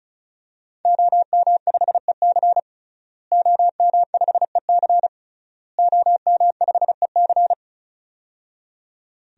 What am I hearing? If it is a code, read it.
Morse code, heard as a keyed tone, about 36 wpm, 700 Hz. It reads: OM5EC OM5EC OM5EC
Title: DXCC-100 Call Signs - 5 Characters - Repeated 3x - 35wpm